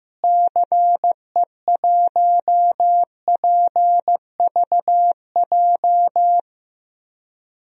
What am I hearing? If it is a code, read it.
Morse code, heard as a keyed tone, about 16 wpm, 700 Hz. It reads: CE1PVJ
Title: DXCC-100 Call Signs - 6 Characters - Repeated 3x - 15wpm